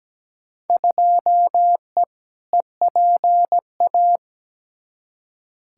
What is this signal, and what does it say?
Morse code, heard as a keyed tone, about 17 wpm, 700 Hz. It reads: 2E EPA
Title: Field Day Exchanges Encore - 17wpm